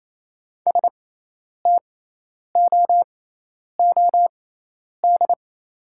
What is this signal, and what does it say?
Morse code, heard as a keyed tone, about 28 wpm, spaced as for 10 wpm, 700 Hz. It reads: STOOD
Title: 6x Character Spacing - Top 500 Words - 28wpm